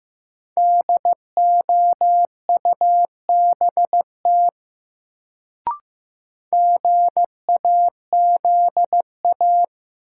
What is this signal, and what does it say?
Morse code, heard as a keyed tone, about 15 wpm, 700 Hz. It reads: DOUBT GAZA
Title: Sets of 5 Words - New Word Drill - 15wpm